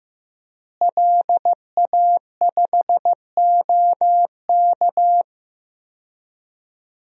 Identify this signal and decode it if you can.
Morse code, heard as a keyed tone, about 15 wpm, 700 Hz. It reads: LA5OK
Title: DXCC-100 Call Signs - 5 Characters - Repeated 3x - 15wpm